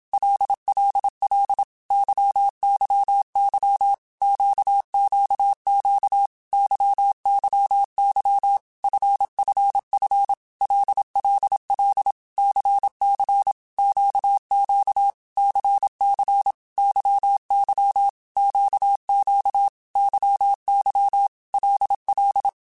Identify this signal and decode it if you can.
Morse code, heard as a keyed tone, about 27 wpm, 800 Hz. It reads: LLL YYY QQQ YYY FFF LLL CC QQ CC YY QQ YY LL
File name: Basic Letter Groups By7Groups / LFCYQ / LFCYQ@25